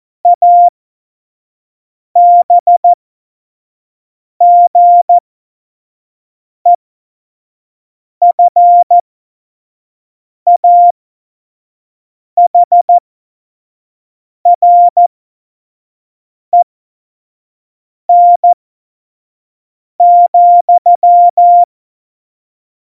Morse code, heard as a keyed tone, about 14 wpm, spaced as for 5 wpm, 700 Hz. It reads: ABGEFAHREN,
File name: Morsekurs / Witze / Witz 12. Februar 2019, 25 F70